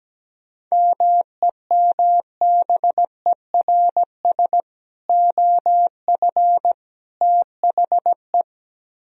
Morse code, heard as a keyed tone, about 17 wpm, 700 Hz. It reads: MEMBERS OF THE
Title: Speed-Racing - Sets of 3 Words 17wpm